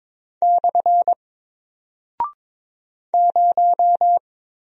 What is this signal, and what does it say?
Morse code, heard as a keyed tone, about 22 wpm, 700 Hz. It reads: / 0
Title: Single Character (Letter:Number:Punctuation:Prosign) Alt 22wpm